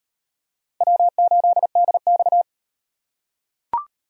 Morse code, heard as a keyed tone, about 36 wpm, 700 Hz. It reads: W8DX